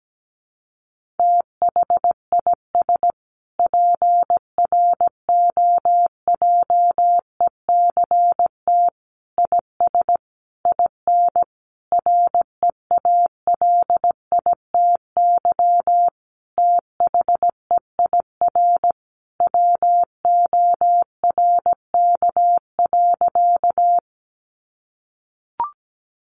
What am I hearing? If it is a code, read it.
Morse code, heard as a keyed tone, about 17 wpm, 700 Hz. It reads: THIS PROJECT IS IN REALITY THEIR WORK.
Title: Sentences from Top 1400 Words 17wpm